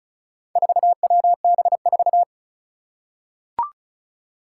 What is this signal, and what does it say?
Morse code, heard as a keyed tone, about 36 wpm, 700 Hz. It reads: 4WB4
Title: Four Challenging Letter-Number Combinations 35wpm